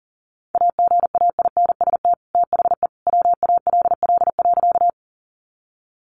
Morse code, heard as a keyed tone, about 40 wpm, 700 Hz. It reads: AGAINST THE WALL.
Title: Sentences from Top 1100 Words 40wpm